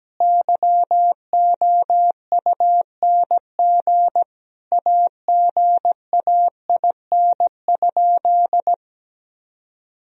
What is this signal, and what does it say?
Morse code, heard as a keyed tone, about 17 wpm, 700 Hz. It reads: YOUNG AGAIN?
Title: Sentences from Top 1000 Words - Review Part 1 - 17wpm